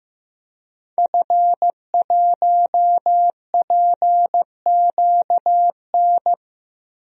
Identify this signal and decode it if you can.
Morse code, heard as a keyed tone, about 15 wpm, 700 Hz. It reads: F1PQN
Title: DXCC-100 Call Signs - 5 Characters - Repeated 3x - 15wpm